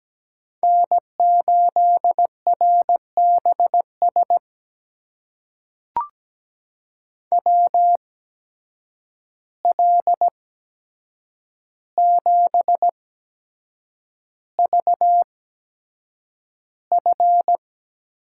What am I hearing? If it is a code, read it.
Morse code, heard as a keyed tone, about 17 wpm, spaced as for 7 wpm, 700 Hz. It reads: N8RBS WL7VF
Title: USA Call Signs - 5 Characters - 8x Character Spacing - 17wpm